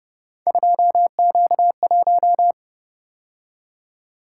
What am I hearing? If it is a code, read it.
Morse code, heard as a keyed tone, about 29 wpm, 700 Hz. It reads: 2Q1